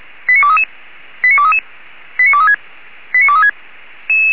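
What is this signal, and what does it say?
Five-tone selective call (ZVEI), heard as quick runs of five tones, tones of about 70 ms. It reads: 89233 89233 89236 89236
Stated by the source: Five-tone ZVEI selective call received by SDR.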